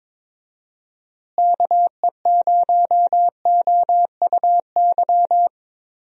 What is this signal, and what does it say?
Morse code, heard as a keyed tone, about 22 wpm, 700 Hz. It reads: KE0OUY
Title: USA Call Signs - 6 Characters - Repeated 3x - 22wpm